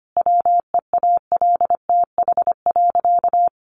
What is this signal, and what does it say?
Morse code, heard as a keyed tone, about 25 wpm, 700 Hz. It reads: WEALTH.
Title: Sentences from Top 1500 Words 25wpm